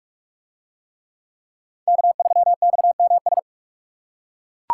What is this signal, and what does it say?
Morse code, heard as a keyed tone, about 39 wpm, 700 Hz. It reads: K3XMS